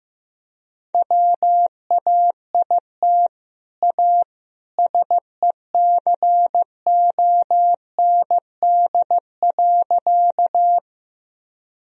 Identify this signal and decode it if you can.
Morse code, heard as a keyed tone, about 15 wpm, 700 Hz. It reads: WAIT A SECOND.
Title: Sentences from Top 1000 Words - Review Part 1 - 15wpm